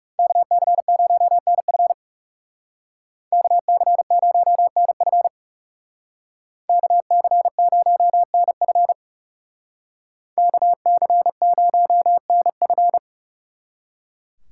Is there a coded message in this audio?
Yes: Morse code from 0.2 to 13.0 s, heard as a keyed tone, about 39 wpm, 700 Hz. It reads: KC0NF KC0NF KC0NF KC0NF